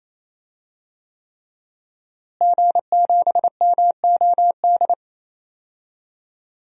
Morse code, heard as a keyed tone, about 27 wpm, 700 Hz. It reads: G7MOD
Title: DXCC-100 Call Signs - 5 Characters - Repeated 3x - 28wpm